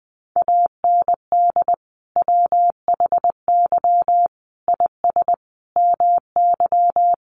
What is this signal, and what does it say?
Morse code, heard as a keyed tone, about 20 wpm, 700 Hz. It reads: AND WHY IS MY
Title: Sentences from Top 1100 Words 20wpm